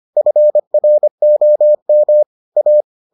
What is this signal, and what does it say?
Morse code, heard as a keyed tone, about 25 wpm, 600 Hz. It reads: FROM A